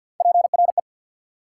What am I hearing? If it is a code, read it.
Morse code, heard as a keyed tone, about 49 wpm, 690 Hz. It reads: PRE
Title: Prefixes and Suffixes 50wpm